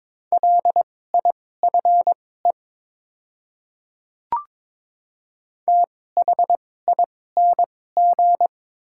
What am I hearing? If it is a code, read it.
Morse code, heard as a keyed tone, about 21 wpm, spaced as for 16 wpm, 700 Hz. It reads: LIFE THING
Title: 2x Character Spacing - Top 500 Words - 22wpm